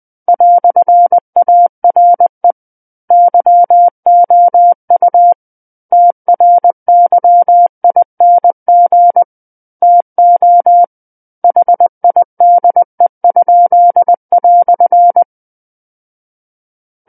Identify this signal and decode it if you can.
Morse code, heard as a keyed tone, about 20 wpm, 700 Hz. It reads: "ARE YOU TRYING TO HIDE?"
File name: 20 WPM